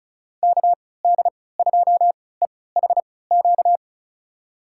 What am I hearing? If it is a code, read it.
Morse code, heard as a keyed tone, about 36 wpm, spaced as for 20 wpm, 700 Hz. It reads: KD2EHQ
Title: USA Call Signs - 6 Characters - 3x Character Spacing - 35wpm